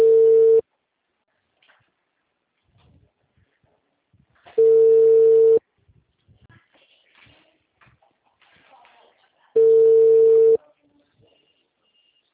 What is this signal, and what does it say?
Ringback tone (China, 450 Hz), with speech.
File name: out- -2002- - - .5393